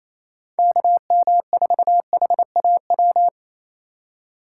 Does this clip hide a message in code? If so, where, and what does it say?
Yes: Morse code from 0.6 to 3.3 s, heard as a keyed tone, about 36 wpm, 700 Hz. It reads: KM4HAW